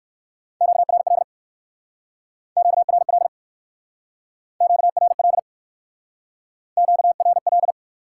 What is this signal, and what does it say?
Morse code, heard as a keyed tone, about 66 wpm, 700 Hz. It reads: QRL QRL QRL QRL